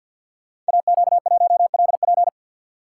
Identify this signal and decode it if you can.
Morse code, heard as a keyed tone, about 48 wpm, 700 Hz. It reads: AQ1LP
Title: Code Groups - 5 Letter:Numbers - 50wpm